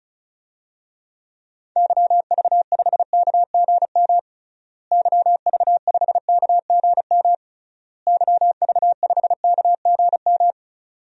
Morse code, heard as a keyed tone, about 36 wpm, 700 Hz. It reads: YV5KGM YV5KGM YV5KGM
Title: DXCC-100 Call Signs - 6 Characters - Repeated 3x - 35wpm